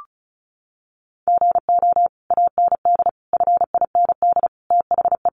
Morse code, heard as a keyed tone, about 34 wpm, 700 Hz. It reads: GO AND FIND THE
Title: Sentences from Top 1200 Words 35wpm